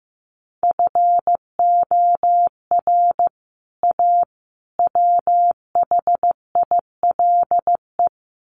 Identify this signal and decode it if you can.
Morse code, heard as a keyed tone, about 15 wpm, 700 Hz. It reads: FOR A WHILE
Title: Sets of 3 Words 15wpm